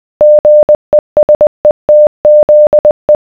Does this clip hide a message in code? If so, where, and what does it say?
Yes: Morse code from 0.2 to 3.2 s, heard as a keyed tone, about 20 wpm, 600 Hz. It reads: GESETZE